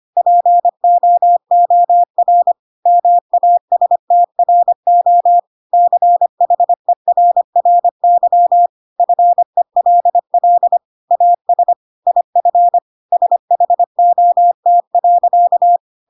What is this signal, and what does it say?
Morse code, heard as a keyed tone, about 25 wpm, 700 Hz. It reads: POOR MASTRO CHERRY FELL AS IF SHOT.
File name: Pinocchio / Pinocchio ch1 25 25 ik1hge